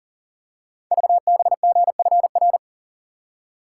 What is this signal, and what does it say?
Morse code, heard as a keyed tone, about 40 wpm, 700 Hz. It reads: VBGFR
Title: Code Groups - 5 Letters - 40wpm